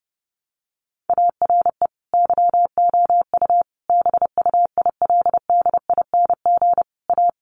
Morse code, heard as a keyed tone, about 30 wpm, 700 Hz. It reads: ARE YOU BUILDING A
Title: Sentences from Top 700 Words 30wpm